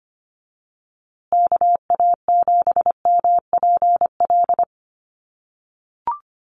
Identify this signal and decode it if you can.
Morse code, heard as a keyed tone, about 25 wpm, 700 Hz. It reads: KA7MPL
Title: USA Call Signs Repeated 2x 25wpm